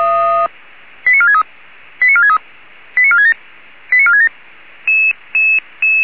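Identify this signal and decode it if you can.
Five-tone selective call (ZVEI), heard as quick runs of five tones, tones of about 70 ms. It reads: 89463 89463 89467 89467
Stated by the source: Five-tone ZVEI selective call received by SDR.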